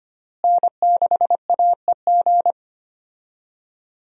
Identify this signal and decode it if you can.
Morse code, heard as a keyed tone, about 25 wpm, 700 Hz. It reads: N6AEG
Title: USA Call Signs - 5 Characters - Repeated 2x - 25wpm